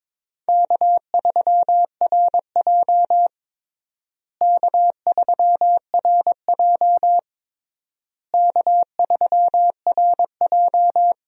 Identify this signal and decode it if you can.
Morse code, heard as a keyed tone, about 22 wpm, 700 Hz. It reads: K3RJ K3RJ K3RJ
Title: USA Call Signs - 4 Characters - Repeated 3x - 22wpm